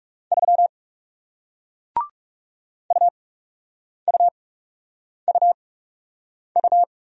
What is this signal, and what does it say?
Morse code, heard as a keyed tone, about 38 wpm, spaced as for 8 wpm, 700 Hz. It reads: 3 UUUU